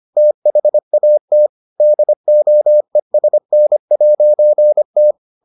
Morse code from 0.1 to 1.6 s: THAT

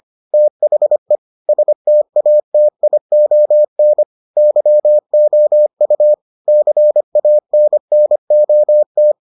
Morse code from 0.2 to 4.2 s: THE STATION

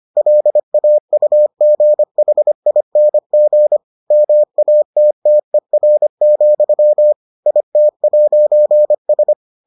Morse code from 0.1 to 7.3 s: LAUGHING MATTER,

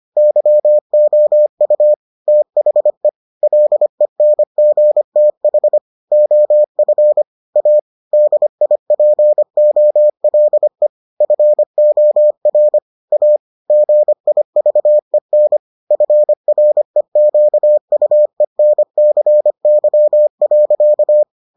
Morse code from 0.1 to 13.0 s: YOU THE LENGTH OF A DIPOLE FOR

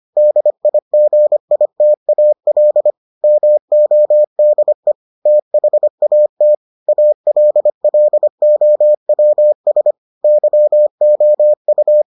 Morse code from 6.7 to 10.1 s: ALLOWS

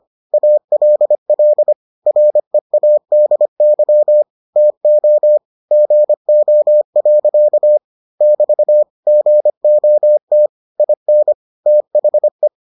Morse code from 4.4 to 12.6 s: TO GO. <BT> GOT IN THE

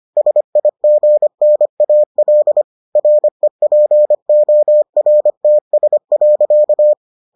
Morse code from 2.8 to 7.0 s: REPORTS.

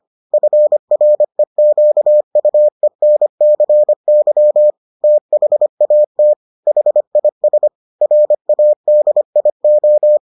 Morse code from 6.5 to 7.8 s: HIS

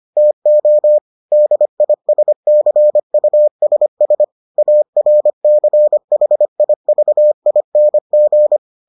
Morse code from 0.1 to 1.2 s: TO

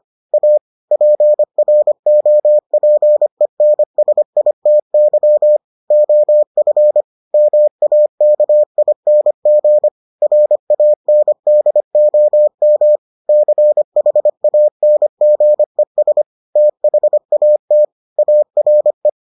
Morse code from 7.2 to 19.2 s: MAKING RANDOM CHANGES THAT ARE

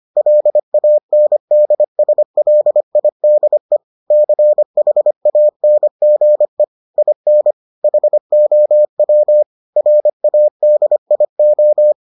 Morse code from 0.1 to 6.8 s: LANDSLIDE CHANGE

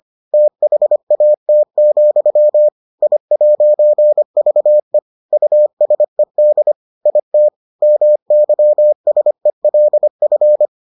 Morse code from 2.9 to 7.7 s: I'VE USED IT

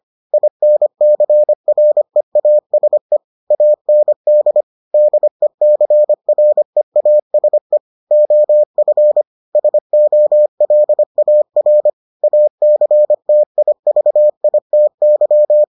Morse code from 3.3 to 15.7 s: AND DECREASE OF SOLAR ACTIVITY

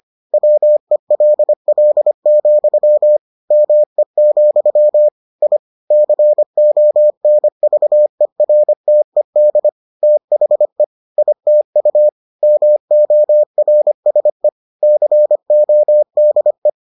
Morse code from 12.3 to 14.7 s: MORSE

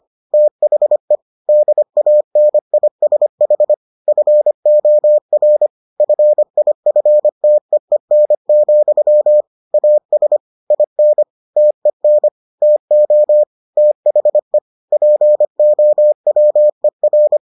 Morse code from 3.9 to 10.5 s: FOR FIFTEEN, AS

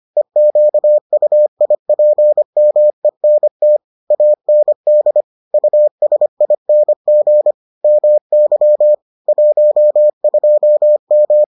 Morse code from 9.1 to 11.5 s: 12M

